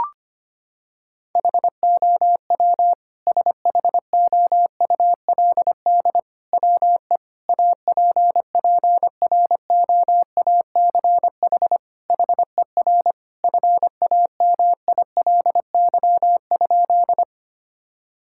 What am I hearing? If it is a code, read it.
Morse code, heard as a keyed tone, about 25 wpm, 700 Hz. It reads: HOW SHOULD WE APPROACH HER FAMILY?
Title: Sentences from Top 1000 Words - Review Part 6 - 25wpm